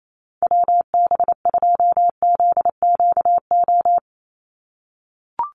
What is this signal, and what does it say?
Morse code, heard as a keyed tone, about 28 wpm, 700 Hz. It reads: WB2ZQO
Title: USA Call Signs 28wpm